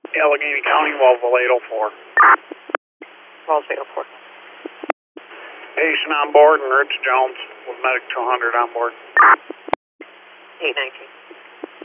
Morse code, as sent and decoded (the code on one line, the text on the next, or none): none